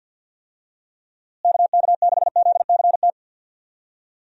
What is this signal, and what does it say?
Morse code, heard as a keyed tone, about 53 wpm, 700 Hz. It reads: KK6ZCT
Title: USA Call Signs - 6 Characters - Repeated 3x - 50wpm